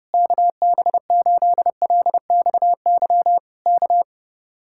Morse code, heard as a keyed tone, about 31 wpm, 700 Hz. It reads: KB8LXY K